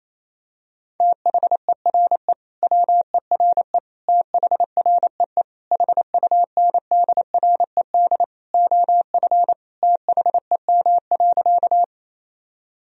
Morse code, heard as a keyed tone, about 28 wpm, 700 Hz. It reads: THERE WERE THREE HUNDRED OF THEM.
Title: Sentences from Top 1000 Words - Review Part 2 - 28wpm